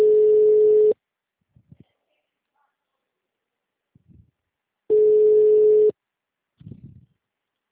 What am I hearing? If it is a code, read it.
Ringback tone (Europe, 425 Hz).
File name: out- -2002- - - .455